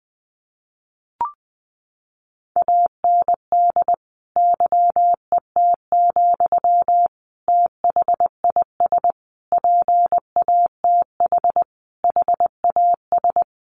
Morse code, heard as a keyed tone, about 20 wpm, 700 Hz. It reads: AND YET, THIS PATH HAS
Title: Sentences from Top 1400 Words 20wpm